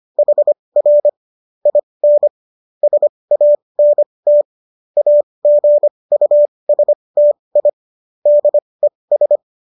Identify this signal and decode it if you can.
Morse code, heard as a keyed tone, about 25 wpm, spaced as for 20 wpm, 590 Hz. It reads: HR IN SANT AGUSTI DES
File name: qso214 20